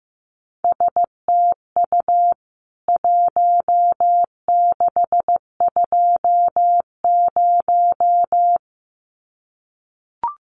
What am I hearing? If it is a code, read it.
Morse code, heard as a keyed tone, about 15 wpm, 700 Hz. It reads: STU 1620
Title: CWT Exchange 15wpm